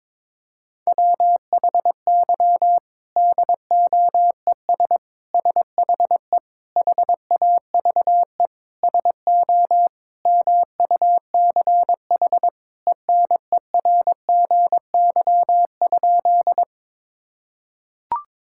Morse code, heard as a keyed tone, about 22 wpm, 700 Hz. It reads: WHY DOES SHE HAVE SO MUCH ENERGY?
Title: Sentences from Top 1000 Words - Review Part 6 - 22wpm